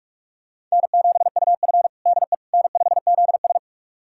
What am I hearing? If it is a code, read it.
Morse code, heard as a keyed tone, about 43 wpm, 700 Hz. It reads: N7UF DE N5ZS
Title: USA Callsign DE Callsign - 4 Characters - 45wpm